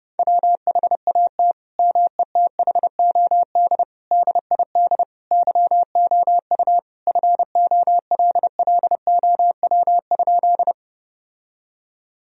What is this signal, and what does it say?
Morse code, heard as a keyed tone, about 32 wpm, 700 Hz. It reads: WHAT METHOD DID YOU FOLLOW?
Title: Sentences from Top 1000 Words - Review Part 3 - 30wpm